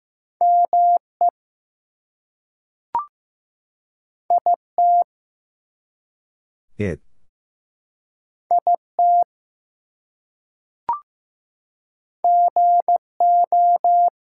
Morse code, heard as a keyed tone, about 15 wpm, 700 Hz. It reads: ME IT IT GO
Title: Two Letter Words - 15wpm